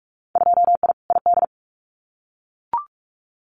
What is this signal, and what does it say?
Morse code, heard as a keyed tone, about 44 wpm, 700 Hz. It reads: 2I ID
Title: Winter Field Day Exchanges Encore 45wpm